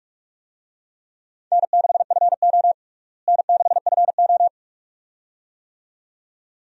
Morse code, heard as a keyed tone, about 46 wpm, 700 Hz. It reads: N6FO N6FO
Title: USA Call Signs - 4 Characters - Repeated 2x - 45wpm